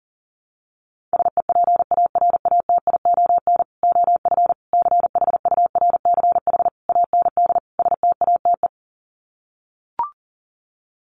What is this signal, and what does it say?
Morse code, heard as a keyed tone, about 40 wpm, 700 Hz. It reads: SEPARATION OF CHURCH AND STATE
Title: Sets of 5 Words 40wpm